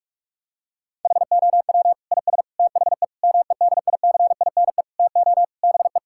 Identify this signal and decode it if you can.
Morse code, heard as a keyed tone, about 48 wpm, 700 Hz. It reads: HOW IS THE MEDICINE TO BE
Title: Sentences from Top 1000 Words - Review Part 6 - 45wpm